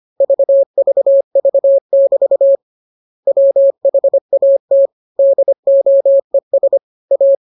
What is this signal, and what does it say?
Morse code, heard as a keyed tone, about 25 wpm, 550 Hz. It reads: VVV<BT> WHAT DOES A
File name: jokes / joke 16 25 0000